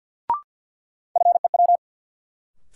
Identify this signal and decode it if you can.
Morse code, heard as a keyed tone, about 47 wpm, 700 Hz. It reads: FEW
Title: Sentences from Top 300 Words - New Word Drill - 50wpm